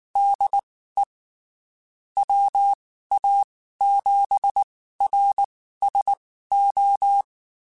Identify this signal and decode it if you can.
Morse code, heard as a keyed tone, about 19 wpm, spaced as for 14 wpm, 800 Hz. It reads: DE WA7RSO
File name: (spaced) / QSOs Spaced-mp3s / kc7sac @18 sp